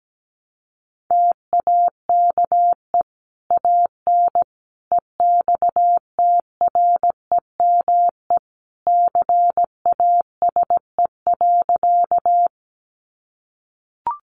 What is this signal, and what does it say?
Morse code, heard as a keyed tone, about 17 wpm, 700 Hz. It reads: TAKE AN EXTREME CASE.
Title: Sentences from Top 1300 Words 17wpm